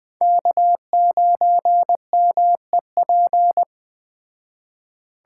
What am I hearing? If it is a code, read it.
Morse code, heard as a keyed tone, about 20 wpm, 700 Hz. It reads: K9MEP
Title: Speed Racing - USA 1x3 Call Signs 17wpm